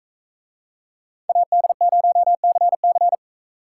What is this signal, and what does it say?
Morse code, heard as a keyed tone, about 44 wpm, 700 Hz. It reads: AD0CC